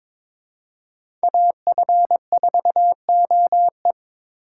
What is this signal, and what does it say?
Morse code, heard as a keyed tone, about 22 wpm, 700 Hz. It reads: AF4OE
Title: USA Call Signs - 4x Character Spacing - 22wpm